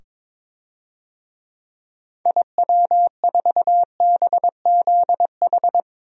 Morse code, heard as a keyed tone, about 22 wpm, 700 Hz. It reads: IW4BZH
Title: DXCC-100 Call Signs - 6 Characters - 22wpm